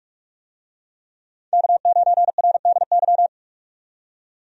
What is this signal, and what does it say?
Morse code, heard as a keyed tone, about 46 wpm, 700 Hz. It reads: K9RDY
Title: USA Call Signs - 5 Characters - 45wpm